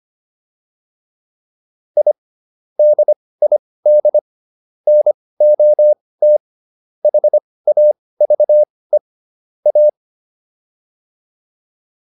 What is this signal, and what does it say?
Morse code, heard as a keyed tone, about 25 wpm, spaced as for 18 wpm, 600 Hz. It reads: I DID NOT HAVE A